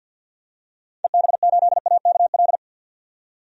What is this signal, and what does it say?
Morse code, heard as a keyed tone, about 50 wpm, 700 Hz. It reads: EB8AKL
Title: DXCC-100 Call Signs - 6 Characters - 50wpm